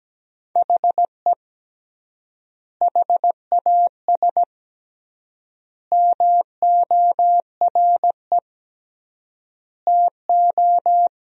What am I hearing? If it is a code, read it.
Morse code, heard as a keyed tone, about 17 wpm, 700 Hz. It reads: HE HAS MORE TO
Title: 3x Word-Spacing - Sentences from Top 200 Words - 17wpm